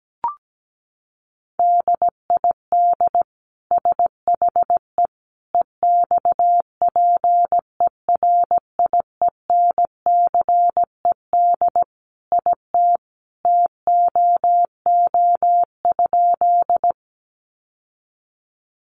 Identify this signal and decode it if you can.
Morse code, heard as a keyed tone, about 17 wpm, 700 Hz. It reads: DID SHE EXPERIENCED IT TOO?
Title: Sentences from Top 1100 Words 17wpm